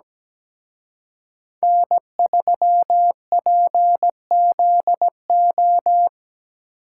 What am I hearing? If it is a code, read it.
Morse code, heard as a keyed tone, about 20 wpm, 700 Hz. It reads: N3PZO
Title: Speed Racing - USA Call Signs 17wpm